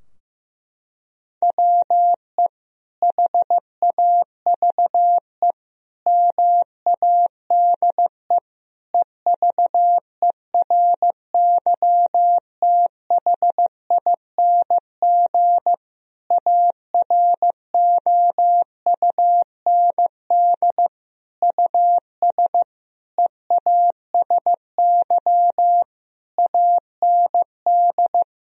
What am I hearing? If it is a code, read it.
Morse code, heard as a keyed tone, about 15 wpm, 700 Hz. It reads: WE HAVE MADE EVERYTHING AROUND US EASY AND
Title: Sentences from Top 1000 Words - Review Part 3 - 15wpm